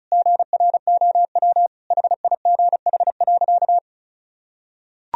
Morse code, heard as a keyed tone, about 34 wpm, 700 Hz. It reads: GROW HIGH.
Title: Sentences from Top 1000 Words - Review Part 1 - 35wpm